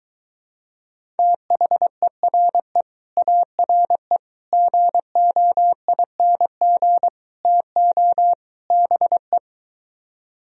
Morse code, heard as a keyed tone, about 23 wpm, 700 Hz. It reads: THERE ARE GOING TO BE